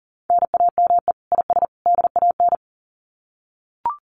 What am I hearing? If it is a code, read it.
Morse code, heard as a keyed tone, about 40 wpm, 700 Hz. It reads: NAME IS DAN
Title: Common QSO Phrases 40wpm